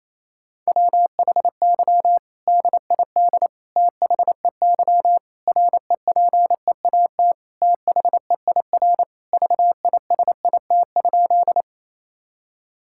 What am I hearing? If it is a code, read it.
Morse code, heard as a keyed tone, about 28 wpm, 700 Hz. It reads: WHY DID THEY REPEAT THEIR VISIT?
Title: Sentences from Top 1600 Words 28wpm